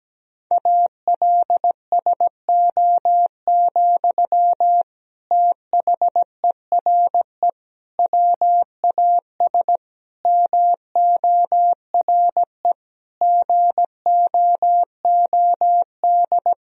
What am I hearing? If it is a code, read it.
Morse code, heard as a keyed tone, about 17 wpm, 700 Hz. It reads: ALSO, THERE WAS MORE GOOD
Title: Sentences from Top 1000 Words - Review Part 1 - 17wpm